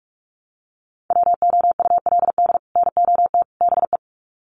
Morse code, heard as a keyed tone, about 44 wpm, 700 Hz. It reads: WOULD NOT BE
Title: Sets of 3 Words 45wpm